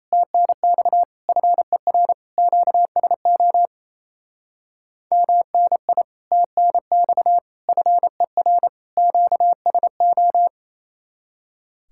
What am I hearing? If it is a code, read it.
Morse code, heard as a keyed tone, about 35 wpm, 700 Hz. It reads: TNX FER QSO MNI TNX FER QSO
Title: Speed-Racing Common QSO Phrases - 28wpm